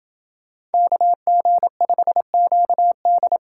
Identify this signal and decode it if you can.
Morse code, heard as a keyed tone, about 28 wpm, 700 Hz. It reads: KG5QD